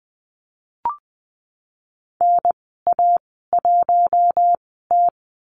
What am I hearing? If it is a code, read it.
Morse code, heard as a keyed tone, about 19 wpm, spaced as for 14 wpm, 700 Hz. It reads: NA1T
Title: 154 Lesson 25g - Letter p - Call Signs 20wpm